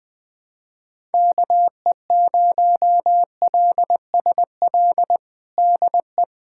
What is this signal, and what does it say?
Morse code, heard as a keyed tone, about 20 wpm, 700 Hz. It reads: KE0LSL DE
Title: USA Callsign DE Callsign 20wpm